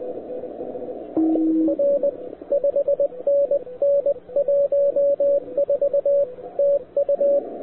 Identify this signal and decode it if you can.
Morse code, heard as a keyed tone, about 20 wpm, 560 Hz, with radio noise. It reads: R 5NN 14 TU
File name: wwcw08 / k5zd wwcw08 1300z